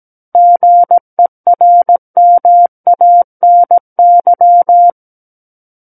Morse code, heard as a keyed tone, about 17 wpm, 700 Hz. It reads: GERMANY
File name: Book One / 08-Chapter 8 - 17 WPM